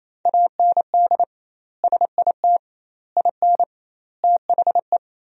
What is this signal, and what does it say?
Morse code, heard as a keyed tone, about 28 wpm, 700 Hz. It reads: AND SIT IN THE
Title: 2x Word-Spacing - Sentences from Top 500 Words - 28wpm